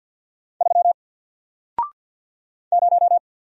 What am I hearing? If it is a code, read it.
Morse code, heard as a keyed tone, about 50 wpm, 700 Hz. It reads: 3 0
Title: Single Letter-Number Alt 50wpm